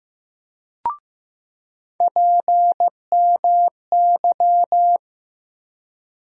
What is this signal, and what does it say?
Morse code, heard as a keyed tone, about 15 wpm, 700 Hz. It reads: PMY
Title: USA Call Sign Postfixes - 3 Letters 15wpm